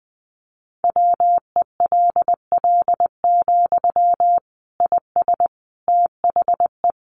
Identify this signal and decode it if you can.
Morse code, heard as a keyed tone, about 19 wpm, 700 Hz. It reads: WELL, IS THE
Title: Sentences from Top 800 Words 20wpm